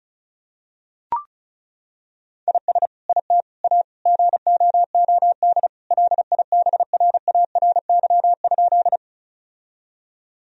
Morse code, heard as a keyed tone, about 36 wpm, 700 Hz. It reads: IS IT A GOOD LIBRARY?
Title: Sentences from Top 1000 Words - Review Part 4 - 35wpm